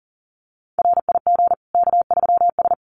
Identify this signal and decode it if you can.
Morse code, heard as a keyed tone, about 40 wpm, 700 Hz. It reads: RIG K3S
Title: Common QSO Phrases 40wpm